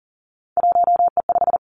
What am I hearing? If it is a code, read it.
Morse code, heard as a keyed tone, about 39 wpm, 700 Hz. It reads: 1E5
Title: Three Challenging Letter-Number Combinations 40wpm